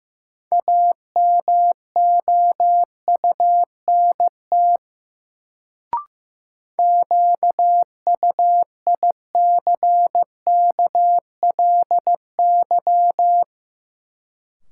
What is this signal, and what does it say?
Morse code, heard as a keyed tone, about 15 wpm, 700 Hz. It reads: AMOUNT QUICKLY
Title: Sentences from Top 700 Words - New Word Drill - 15wpm